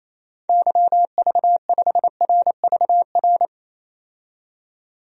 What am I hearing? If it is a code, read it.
Morse code, heard as a keyed tone, about 28 wpm, 700 Hz. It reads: YV5RVR
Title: DXCC-100 Call Signs - 6 Characters - 28wpm